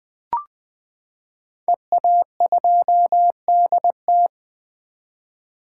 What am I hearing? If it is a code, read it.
Morse code, heard as a keyed tone, about 20 wpm, 700 Hz. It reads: EA2DT
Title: 2023 SOTA Chaser Call Signs - 20wpm